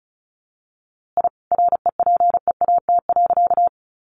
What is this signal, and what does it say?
Morse code, heard as a keyed tone, about 34 wpm, 700 Hz. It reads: I REPEAT.
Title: A Sound Of Thunder 35wpm